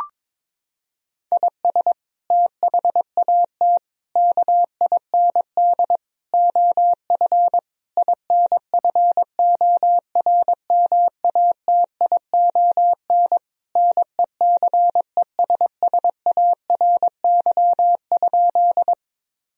Morse code, heard as a keyed tone, about 22 wpm, 700 Hz. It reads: IS THAT KIND OF INFORMATION NECESSARY?
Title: Sentences from Top 1000 Words - Review Part 3 - 22wpm